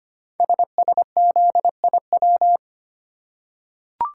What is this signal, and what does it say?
Morse code, heard as a keyed tone, about 25 wpm, 700 Hz. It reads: SSZIW
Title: Code Groups - 5 Letters - 25wpm